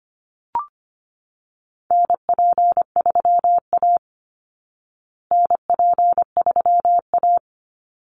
Morse code, heard as a keyed tone, about 25 wpm, 700 Hz. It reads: NP3A NP3A
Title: Call Signs Repeated 3x - 25wpm